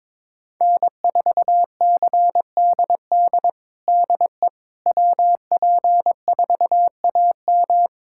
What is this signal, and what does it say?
Morse code, heard as a keyed tone, about 22 wpm, 700 Hz. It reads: N4CDD DE WP4AM
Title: USA Callsign DE Callsign - 5 Characters - 22wpm